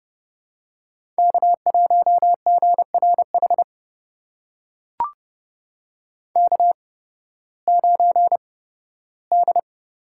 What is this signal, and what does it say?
Morse code, heard as a keyed tone, about 32 wpm, spaced as for 16 wpm, 700 Hz. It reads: K1GRH K9D
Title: USA Call Signs - 5 Characters - 8x Character Spacing - 30wpm